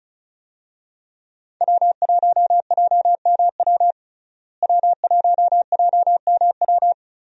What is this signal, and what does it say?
Morse code, heard as a keyed tone, about 34 wpm, 700 Hz. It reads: W1JMW W1JMW
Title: USA Call Signs - 5 Characters - Repeated 3x - 35wpm